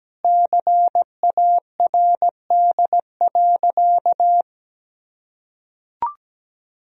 Morse code, heard as a keyed tone, about 17 wpm, 700 Hz. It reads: CARD.
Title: Sentences from Top 1000 Words - Review Part 4 - 17wpm